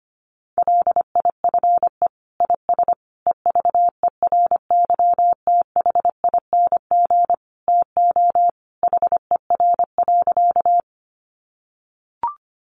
Morse code, heard as a keyed tone, about 25 wpm, 700 Hz. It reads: LIFE IS EVERYTHING TO HER.
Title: Sentences from Top 700 Words 25wpm